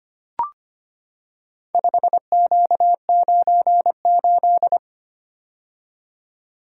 Morse code, heard as a keyed tone, about 25 wpm, 700 Hz. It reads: 5Q98
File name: practice / 25wpm / Four Letter-Number Combinations - 25wpm